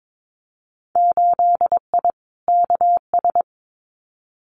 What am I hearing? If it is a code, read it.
Morse code, heard as a keyed tone, about 22 wpm, 700 Hz. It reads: 8I KS
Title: Winter Field Day Exchanges Encore 22wpm